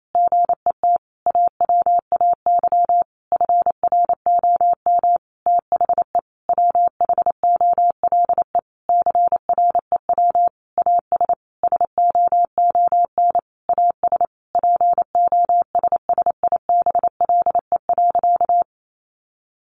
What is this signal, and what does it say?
Morse code, heard as a keyed tone, about 28 wpm, 700 Hz. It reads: GET AWAY FROM THE WHOLE CREW AS SOON AS POSSIBLE.
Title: Sentences from Top 1500 Words 28wpm